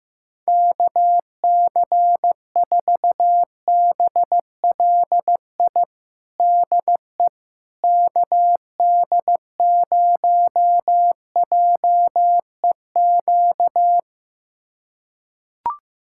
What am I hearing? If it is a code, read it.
Morse code, heard as a keyed tone, about 15 wpm, 700 Hz. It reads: KC4BLI DE KD0JEQ
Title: USA Callsign DE Callsign 15wpm